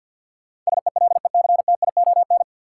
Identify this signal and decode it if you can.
Morse code, heard as a keyed tone, about 53 wpm, 700 Hz. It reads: SELECTION
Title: Sentences from Top 1400 Words - New Word Drill - 50wpm